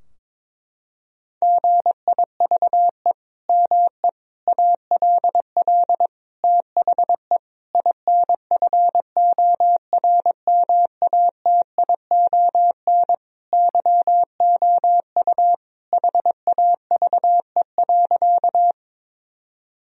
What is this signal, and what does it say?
Morse code, heard as a keyed tone, about 22 wpm, 700 Hz. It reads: GIVE ME ALL THE INFORMATION YOU HAVE.
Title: Sentences from Top 1000 Words - Review Part 3 - 22wpm